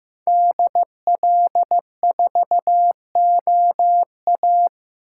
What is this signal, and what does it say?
Morse code, heard as a keyed tone, about 18 wpm, 700 Hz. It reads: DL4OA
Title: Speed Racing - USA 2x2 Call Signs 15wpm